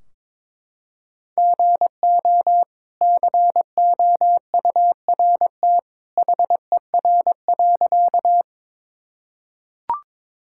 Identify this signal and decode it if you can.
Morse code, heard as a keyed tone, about 22 wpm, 700 Hz. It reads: GO COURT HER.
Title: Sentences from Top 1600 Words 22wpm